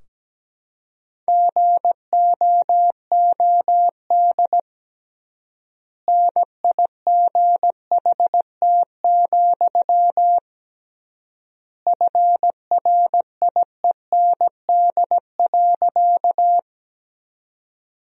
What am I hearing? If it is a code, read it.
Morse code, heard as a keyed tone, about 17 wpm, 700 Hz. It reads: GOOD NIGHT, FRIEND.
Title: 3x Word-Spacing - Sentences from Top 300 Words - 17wpm